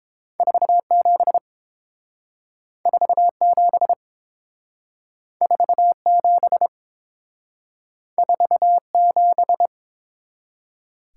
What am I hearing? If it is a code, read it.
Morse code, heard as a keyed tone, about 30 wpm, 700 Hz. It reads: 47 47 47 47